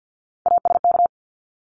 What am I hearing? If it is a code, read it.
Morse code, heard as a keyed tone, about 50 wpm, 700 Hz. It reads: ASK
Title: Top 300 Words Repeated 50wpm